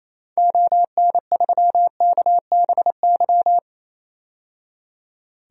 Morse code, heard as a keyed tone, about 28 wpm, 700 Hz. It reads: ON3KBY
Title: DXCC-100 Call Signs - 6 Characters - 28wpm